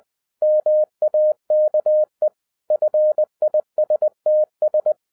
Morse code, heard as a keyed tone, about 20 wpm, 610 Hz. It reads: MAKE FISTS